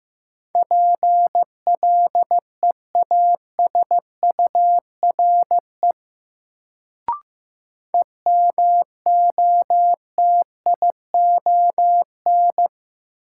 Morse code, heard as a keyed tone, about 15 wpm, 700 Hz. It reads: PLEASURE EMOTION
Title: Sentences from Top 1600 Words - New Word Drill Alt - 15wpm